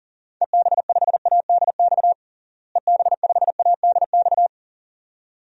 Morse code, heard as a keyed tone, about 40 wpm, 700 Hz. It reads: EB5ADX EB5ADX
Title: DXCC-100 Call Signs - 6 Characters - Repeated 2x - 40wpm